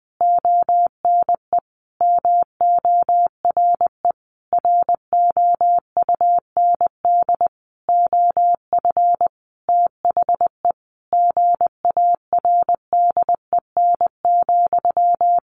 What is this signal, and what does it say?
Morse code, heard as a keyed tone, about 20 wpm, 700 Hz. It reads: ONE MORE ROUND OF THE GARDEN,
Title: Sentences from Top 700 Words 20wpm